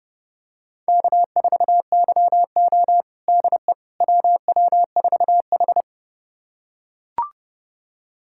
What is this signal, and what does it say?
Morse code, heard as a keyed tone, about 30 wpm, 700 Hz. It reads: K4YO DE WW4H
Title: USA Callsign DE Callsign - 4 Characters - 30wpm